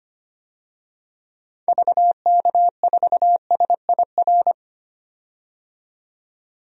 Morse code, heard as a keyed tone, about 25 wpm, 700 Hz. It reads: VK4SIR
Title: DXCC-100 Call Signs - 6 Characters - 25wpm